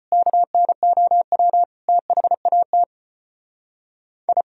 Morse code, heard as a keyed tone, about 33 wpm, 700 Hz. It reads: KNOW THAT I